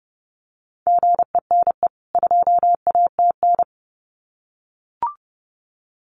Morse code, heard as a keyed tone, about 30 wpm, 700 Hz. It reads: GENE 2ATN
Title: CWT Exchange 30wpm